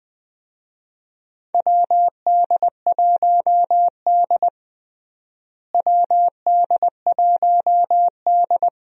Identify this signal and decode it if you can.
Morse code, heard as a keyed tone, about 20 wpm, 700 Hz. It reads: WD1D WD1D
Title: USA Call Signs - 4 Characters - Repeated 2x - 20wpm